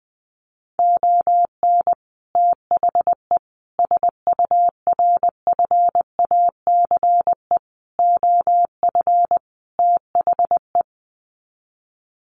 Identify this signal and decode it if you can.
Morse code, heard as a keyed tone, about 20 wpm, 700 Hz. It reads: ON THE SURFACE OF THE
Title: Sets of 5 Words Encore 20wpm